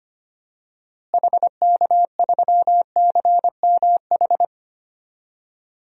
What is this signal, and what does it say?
Morse code, heard as a keyed tone, about 25 wpm, 700 Hz. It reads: HK3CMH
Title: DXCC-100 Call Signs - 6 Characters - Repeated 3x - 25wpm